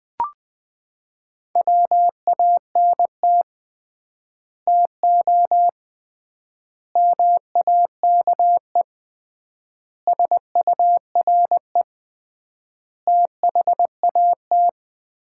Morse code, heard as a keyed tone, about 20 wpm, 700 Hz. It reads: WANT TO MAKE SURE THAT
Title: 3x Word-Spacing - Sets of 5 Words - 20wpm